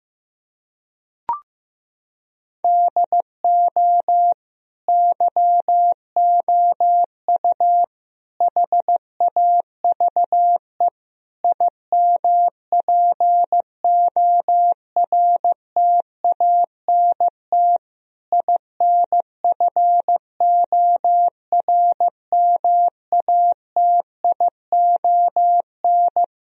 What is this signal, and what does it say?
Morse code, heard as a keyed tone, about 16 wpm, 700 Hz. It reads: DO YOU HAVE IMPORTANT INFORMATION
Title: Sentences from Top 1000 Words - Review Part 3 - 15wpm